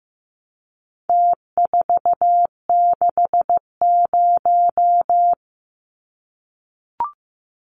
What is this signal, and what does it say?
Morse code, heard as a keyed tone, about 15 wpm, 700 Hz. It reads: T460
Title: Numbers - Contest Style - 4-Digit 15wpm